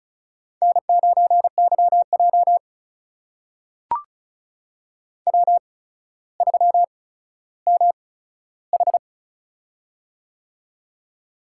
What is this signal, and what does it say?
Morse code, heard as a keyed tone, about 36 wpm, spaced as for 15 wpm, 700 Hz. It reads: N9YJ W3MH
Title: USA Call Signs - 4 Characters - 8x Character Spacing - 35wpm